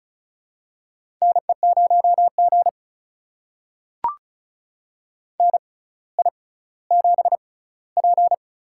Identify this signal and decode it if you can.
Morse code, heard as a keyed tone, about 34 wpm, spaced as for 18 wpm, 700 Hz. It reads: NE0G NI7P
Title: USA Call Signs - 4 Characters - 6x Character Spacing - 35wpm